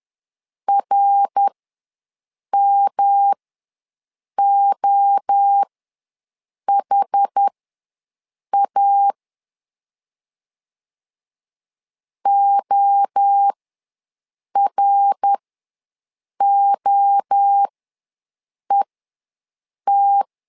Morse code, heard as a keyed tone, about 10 wpm, spaced as for 6 wpm, 800 Hz. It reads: RMOHA OROET
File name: LEZIONI / 3 MWRO / 3 MWRO ws 10wpm 2310